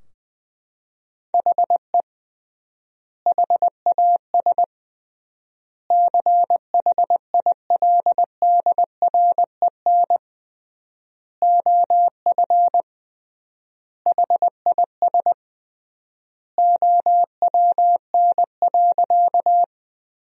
Morse code, heard as a keyed tone, about 20 wpm, 700 Hz. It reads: HE HAS CHILDREN OF HIS OWN.
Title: 3x Word-Spacing - Sentences from Top 300 Words - 20wpm